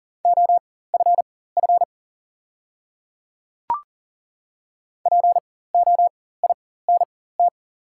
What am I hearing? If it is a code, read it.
Morse code, heard as a keyed tone, about 40 wpm, spaced as for 19 wpm, 700 Hz. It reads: OFF POINT
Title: 4x Character Spacing - Top 200 Words - 40wpm